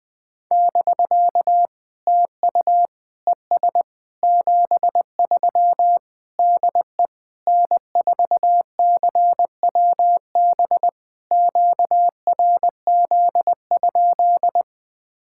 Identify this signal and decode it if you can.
Morse code, heard as a keyed tone, about 20 wpm, 690 Hz. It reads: <BK> TU ES 73 DE N4CWB QRZ?
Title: POTA QSO Practice - 20wpm